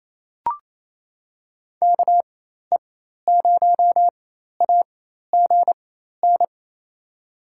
Morse code, heard as a keyed tone, about 28 wpm, spaced as for 13 wpm, 700 Hz. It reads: KE0AGN
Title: USA Call Signs - 6 Characters - 4x Character Spacing - 28wpm